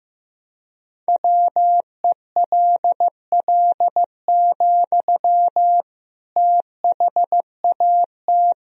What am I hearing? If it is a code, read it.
Morse code, heard as a keyed tone, about 15 wpm, 700 Hz. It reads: WELL, THAT
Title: Sentences from Top 1000 Words - Review Part 2 - 15wpm